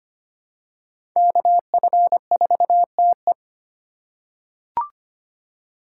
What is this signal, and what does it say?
Morse code, heard as a keyed tone, about 25 wpm, 700 Hz. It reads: KF4TE
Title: USA Call Signs - 5 Characters - 2x Character Spacing - 25wpm